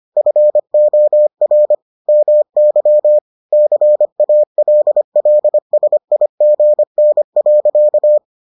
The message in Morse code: FOR MY CALLSIGN.

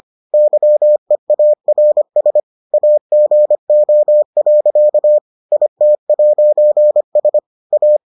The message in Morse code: YEARS AGO. IT'S A

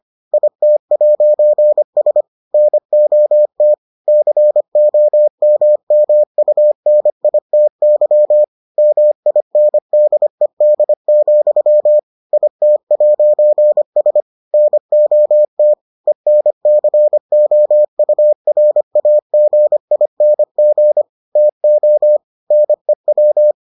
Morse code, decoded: IT'S NOT COMMUNITY MINDED, IT'S NOT ENCOURAGING TO NEW

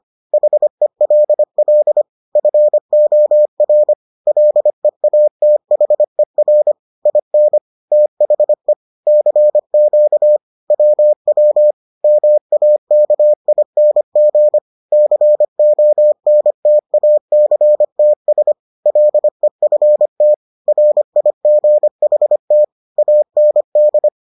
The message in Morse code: HELL FOR LEATHER IN THE CQ WW MAKING CONTACTS LEFT RIGHT AND

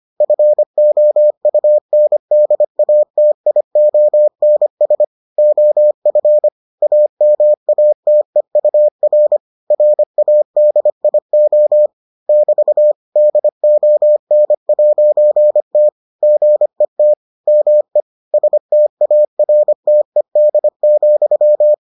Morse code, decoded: FOUNDATIONS OF AMATEUR RADIO <BT> DON'T GET ME STARTED,

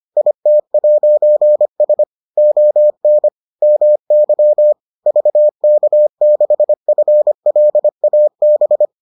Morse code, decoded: IT'S ON MY VK6FLAB

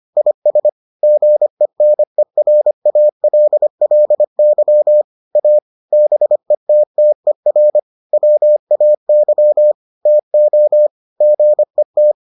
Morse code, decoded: IS GENERALLY A BETTER WAY TO GET

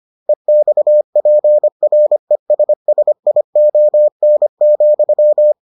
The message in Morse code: EXPRESSION,

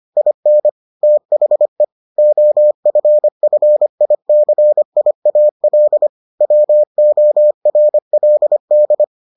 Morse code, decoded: IN THE OFFICIAL WORLD

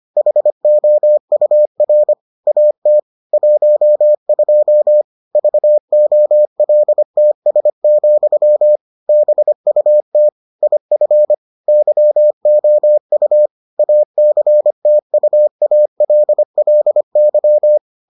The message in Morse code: HOUR AT 12 VOLTS, BUT IF YOU ACTUALLY

